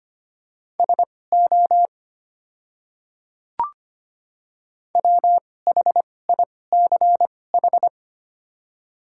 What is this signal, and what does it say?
Morse code, heard as a keyed tone, about 25 wpm, spaced as for 18 wpm, 690 Hz. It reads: SO WHICH
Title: 2x Character Spacing - Top 100 Words - 25wpm